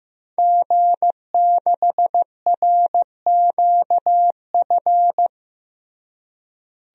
Morse code, heard as a keyed tone, about 15 wpm, 700 Hz. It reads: G6RQF
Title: DXCC-100 Call Signs - 5 Characters - Repeated 3x - 15wpm